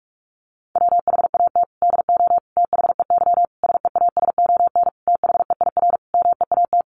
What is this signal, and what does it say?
Morse code, heard as a keyed tone, about 44 wpm, 700 Hz. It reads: WHAT DO THEY SEASON THEIR MEAT
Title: Sentences from Top 1400 Words 45wpm